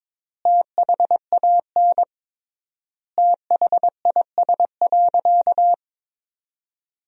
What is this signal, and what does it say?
Morse code, heard as a keyed tone, about 22 wpm, 700 Hz. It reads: THAN THIS.
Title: 3x Word-Spacing - Sentences from Top 100 Words - 22wpm